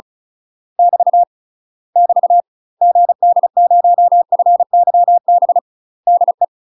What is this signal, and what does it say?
Morse code, heard as a keyed tone, about 34 wpm, 700 Hz. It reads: <BT> <BT> GD0FYB DE
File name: UK MASTER QSOs TRAINING / 35WPM / UK MASTER QSOs 0021